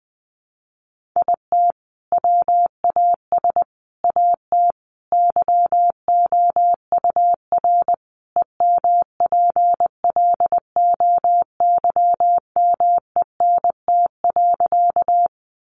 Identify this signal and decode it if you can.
Morse code, heard as a keyed tone, about 20 wpm, 700 Hz. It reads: IT WAS AT YOUR EMPLOYMENT.
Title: Sentences from Top 1500 Words 20wpm